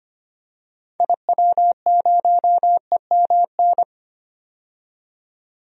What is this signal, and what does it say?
Morse code, heard as a keyed tone, about 25 wpm, 700 Hz. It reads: IW0EMN
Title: DXCC-100 Call Signs - 6 Characters - 25wpm